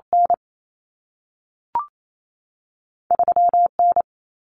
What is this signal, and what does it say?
Morse code, heard as a keyed tone, about 28 wpm, spaced as for 3 wpm, 700 Hz. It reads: N3 N
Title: Common Cut Numbers - 2-Digit 28wpm